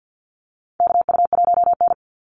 Morse code, heard as a keyed tone, about 50 wpm, 700 Hz. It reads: KU1N
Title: Call Signs Repeated 3x - 50wpm